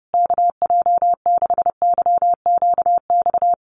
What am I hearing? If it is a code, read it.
Morse code, heard as a keyed tone, about 30 wpm, 700 Hz. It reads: KJ6YQX
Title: USA Call Signs Repeated 3x 30wpm